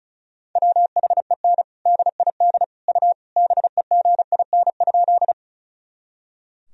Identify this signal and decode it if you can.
Morse code, heard as a keyed tone, about 34 wpm, 700 Hz. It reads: WHEN DID U BEGIN?
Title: Sentences from Top 300 Words and Abbreviations 35wpm